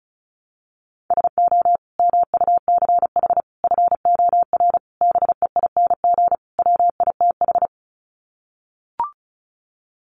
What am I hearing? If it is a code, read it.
Morse code, heard as a keyed tone, about 34 wpm, 700 Hz. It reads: SO MUCH FOR BEING WITH
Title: Sets of 5 Words Encore 35wpm